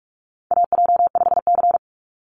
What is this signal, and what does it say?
Morse code, heard as a keyed tone, about 44 wpm, 700 Hz. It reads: AJ5C
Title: NA SOTA Chaser Callsigns 45wpm